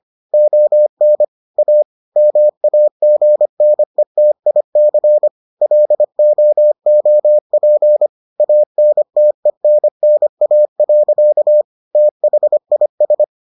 Morse code, heard as a keyed tone, about 25 wpm, 600 Hz. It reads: ON A MAGNETIC LOOP ANTENNA. THIS